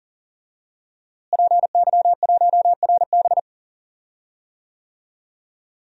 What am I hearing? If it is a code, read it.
Morse code, heard as a keyed tone, about 37 wpm, 700 Hz. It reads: PY1RB
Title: DXCC-100 Call Signs - 5 Characters - Repeated 2x - 40wpm